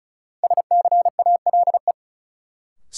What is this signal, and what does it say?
Morse code, heard as a keyed tone, about 36 wpm, 700 Hz. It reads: SCALE
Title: Sentences from Top 1000 Words - Review Part 4 - New Word Drill - 35wpm